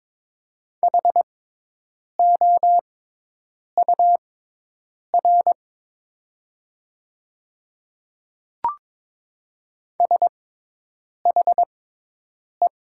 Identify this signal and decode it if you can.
Morse code, heard as a keyed tone, about 22 wpm, spaced as for 8 wpm, 700 Hz. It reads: HOUR SHE
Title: 6x Character Spacing - Top 500 Words - 22wpm